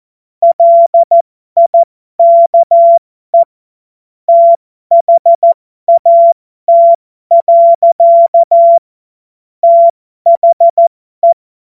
Morse code, heard as a keyed tone, about 14 wpm, spaced as for 12 wpm, 680 Hz. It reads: LIKE THAT. THE
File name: Morsekurs / Witze